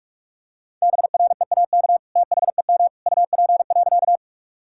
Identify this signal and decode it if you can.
Morse code, heard as a keyed tone, about 46 wpm, 700 Hz. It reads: BREAK THEM UP.
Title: Sentences from Top 1000 Words - Review Part 6 - 45wpm